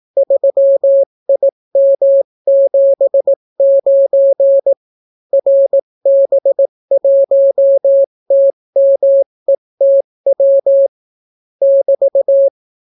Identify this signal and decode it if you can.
Morse code, heard as a keyed tone, about 18 wpm, 550 Hz. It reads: 3IM79 RB1TMETW <BT>